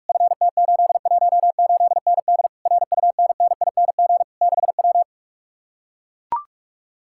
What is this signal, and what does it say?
Morse code, heard as a keyed tone, about 44 wpm, 700 Hz. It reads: FT818ND RUNNING 6W